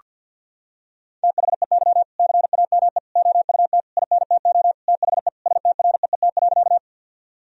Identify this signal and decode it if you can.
Morse code, heard as a keyed tone, about 50 wpm, 700 Hz. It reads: THEY CAME OUT INTO THE STREET.
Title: Sentences from Top 1000 Words - Review Part 3 - 50wpm